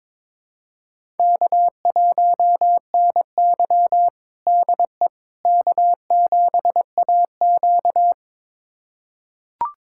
Morse code, heard as a keyed tone, about 22 wpm, 700 Hz. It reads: K1NY DE K7AQ
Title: USA Callsign DE Callsign - 4 Characters - 22wpm